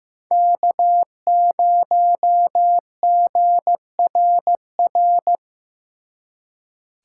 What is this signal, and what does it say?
Morse code, heard as a keyed tone, about 17 wpm, 700 Hz. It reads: K0GRR
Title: Speed Racing - USA 1x3 Call Signs 15wpm